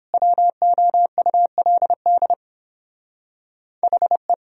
Morse code, heard as a keyed tone, about 30 wpm, 700 Hz. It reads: WOULD HE